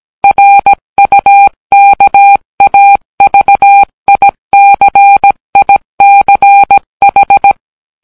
Morse code, heard as a keyed tone, about 17 wpm, 800 Hz. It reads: RUKAVICICH